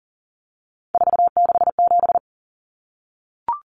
Morse code, heard as a keyed tone, about 40 wpm, 700 Hz. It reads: 467
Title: Common Cut Numbers - 3-Digit 40wpm